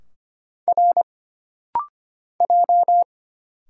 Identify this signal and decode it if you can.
Morse code, heard as a keyed tone, about 25 wpm, 700 Hz. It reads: R J
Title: Single Letters - Rapid-Fire 25wpm